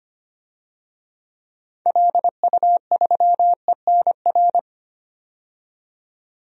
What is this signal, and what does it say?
Morse code, heard as a keyed tone, about 25 wpm, 700 Hz. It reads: LU3ENR
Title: DXCC-100 Call Signs - 6 Characters - 25wpm